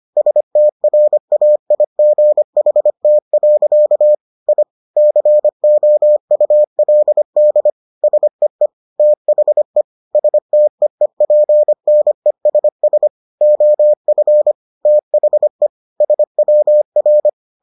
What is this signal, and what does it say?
Morse code, heard as a keyed tone, about 25 wpm, 600 Hz. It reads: STRAIGHT. I COULD SEE THE STEEPNESS OF THE SWR